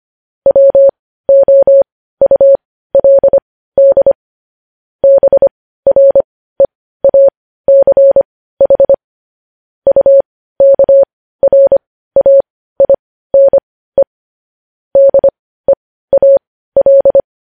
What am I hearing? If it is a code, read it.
Morse code, heard as a keyed tone, about 25 wpm, spaced as for 15 wpm, 550 Hz. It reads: WOULD BREACH UKRAINE DEAL
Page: news-headlines-in-morse-code-at-15-wpm